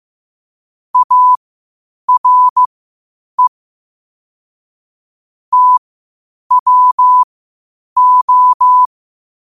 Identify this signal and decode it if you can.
Morse code, heard as a keyed tone, about 15 wpm, spaced as for 8 wpm, 1000 Hz. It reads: ARE TWO
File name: qotdfiles / 8wpm